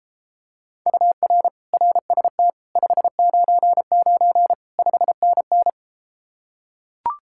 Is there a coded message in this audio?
Yes: Morse code from 0.9 to 5.7 s, heard as a keyed tone, about 33 wpm, 700 Hz. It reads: UR RST 599 5NN